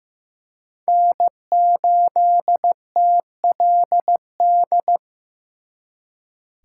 Morse code, heard as a keyed tone, about 18 wpm, 700 Hz. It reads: N8TLD
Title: Speed Racing - USA 1x3 Call Signs 15wpm